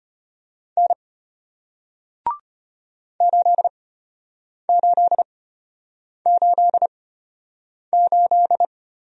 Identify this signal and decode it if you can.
Morse code, heard as a keyed tone, about 33 wpm, spaced as for 8 wpm, 700 Hz. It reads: N 8888